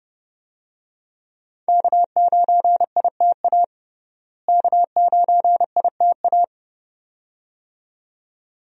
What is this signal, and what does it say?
Morse code, heard as a keyed tone, about 32 wpm, 700 Hz. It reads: K9ITA K9ITA
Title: USA Call Signs - 5 Characters - Repeated 2x - 30wpm